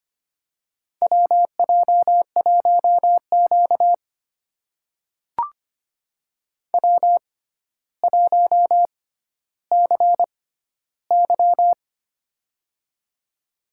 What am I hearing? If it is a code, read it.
Morse code, heard as a keyed tone, about 25 wpm, spaced as for 13 wpm, 700 Hz. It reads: WJ1Q W1CY
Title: USA Call Signs - 4 Characters - 6x Character Spacing - 25wpm